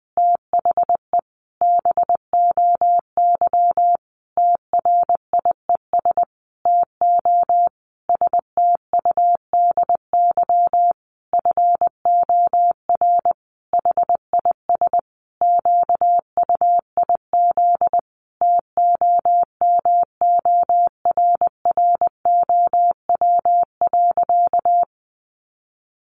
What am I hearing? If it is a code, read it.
Morse code, heard as a keyed tone, about 20 wpm, 700 Hz. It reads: THE BOY TRIES TO STUDY FOR HIS QUIZ TOMORROW.
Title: Additional Easy Phrases 20wpm